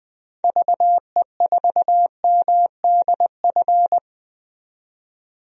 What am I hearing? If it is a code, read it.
Morse code, heard as a keyed tone, about 20 wpm, 700 Hz. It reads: VE4MDF
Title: DXCC-100 Call Signs - 6 Characters - 20wpm